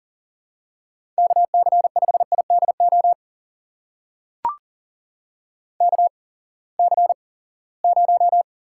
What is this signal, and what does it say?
Morse code, heard as a keyed tone, about 40 wpm, spaced as for 23 wpm, 700 Hz. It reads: KC5IDO KC0
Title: USA Call Signs - 6 Characters - 8x Character Spacing - 40wpm